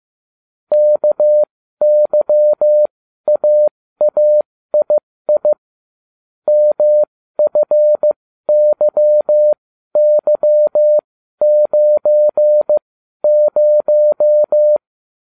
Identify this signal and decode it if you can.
Morse code, heard as a keyed tone, about 15 wpm, spaced as for 12 wpm, 600 Hz. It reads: KYAAII MFYY90